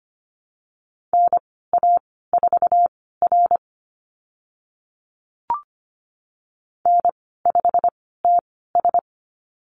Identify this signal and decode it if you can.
Morse code, heard as a keyed tone, about 25 wpm, spaced as for 16 wpm, 700 Hz. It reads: NA4R N5TS
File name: Lesson 15g - Number 5 - Call Signs 25wpm